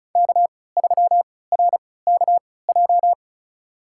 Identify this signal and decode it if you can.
Morse code, heard as a keyed tone, about 36 wpm, spaced as for 20 wpm, 700 Hz. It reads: K3RKJ
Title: USA Call Signs - 5 Characters - 3x Character Spacing - 35wpm